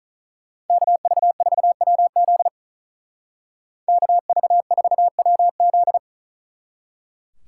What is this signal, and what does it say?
Morse code, heard as a keyed tone, about 47 wpm, 700 Hz. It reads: KV4WZ KV4WZ